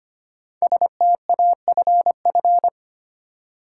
Morse code, heard as a keyed tone, about 26 wpm, 700 Hz. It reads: STAFF
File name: 5 Letter Words - First 3 Letters Given - 25wpm